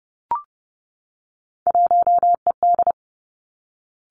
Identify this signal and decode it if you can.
Morse code, heard as a keyed tone, about 30 wpm, 700 Hz. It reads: 1ED
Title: Three Challenging Letter-Number Combinations 30wpm